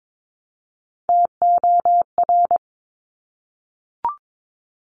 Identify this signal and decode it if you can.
Morse code, heard as a keyed tone, about 22 wpm, 700 Hz. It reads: TOR
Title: Top Three Letter Combinations in Words 22wpm